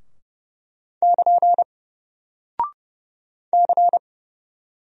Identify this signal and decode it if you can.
Morse code, heard as a keyed tone, about 30 wpm, 700 Hz. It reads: <KN> C
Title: Single Character (Letter:Number:Punctuation:Prosign) Alt 30wpm